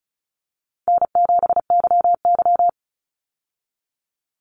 Morse code, heard as a keyed tone, about 34 wpm, 700 Hz. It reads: N7YY
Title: Easy Call Signs Repeated 2x - 35wpm